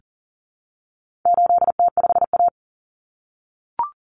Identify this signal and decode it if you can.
Morse code, heard as a keyed tone, about 40 wpm, 700 Hz. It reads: 8T5A
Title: Common Cut Numbers - 4-Digit 40wpm